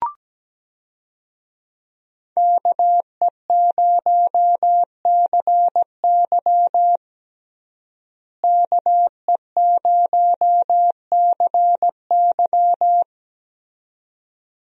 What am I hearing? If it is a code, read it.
Morse code, heard as a keyed tone, about 17 wpm, 700 Hz. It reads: KE0CY KE0CY
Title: USA Call Signs - 5 Characters - Repeated 2x - 17wpm